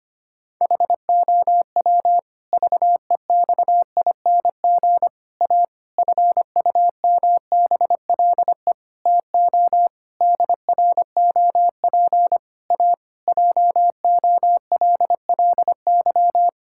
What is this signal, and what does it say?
Morse code, heard as a keyed tone, about 25 wpm, 700 Hz. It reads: HOW VEXING A FUMBLE TO DROP A JOLLY
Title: Pangram Challenge v2 - 25wpm